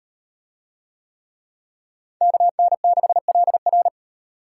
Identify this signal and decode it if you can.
Morse code, heard as a keyed tone, about 38 wpm, 700 Hz. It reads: KN6LR